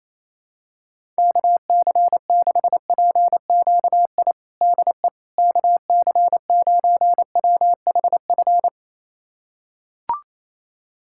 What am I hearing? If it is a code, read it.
Morse code, heard as a keyed tone, about 27 wpm, 700 Hz. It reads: KC6PQI DE KC9WHF
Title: USA Callsign DE Callsign - 6 Characters - 28wpm